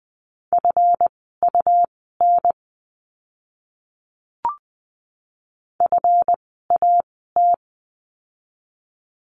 Morse code, heard as a keyed tone, about 20 wpm, spaced as for 15 wpm, 700 Hz. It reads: FUN FAT
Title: 139 Lesson 23f - Letter f - Words 20wpm